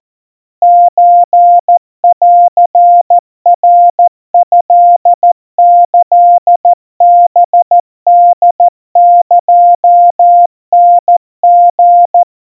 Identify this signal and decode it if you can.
Morse code (Japanese Wabun code), heard as a keyed tone, about 14 wpm, 700 Hz. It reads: ソンナトキハホエタリ
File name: kawa / Morse sample / kamyu 50 cpm 700 Hz Crispy